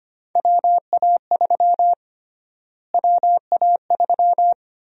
Morse code, heard as a keyed tone, about 25 wpm, 700 Hz. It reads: WA3 WA3
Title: USA Call Sign Prefixes - Repeated 2x - 25wpm